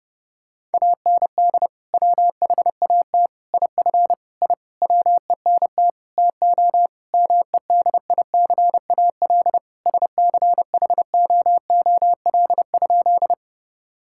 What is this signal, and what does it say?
Morse code, heard as a keyed tone, about 32 wpm, 700 Hz. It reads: AND WHAT IF I WENT TO MEDICAL SCHOOL?
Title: Sentences from Top 1000 Words - Review Part 5 - 30wpm